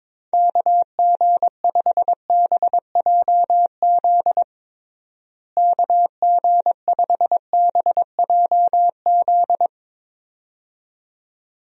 Morse code, heard as a keyed tone, about 22 wpm, 700 Hz. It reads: KG5BJZ KG5BJZ
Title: USA Call Signs - 6 Characters - Repeated 2x - 22wpm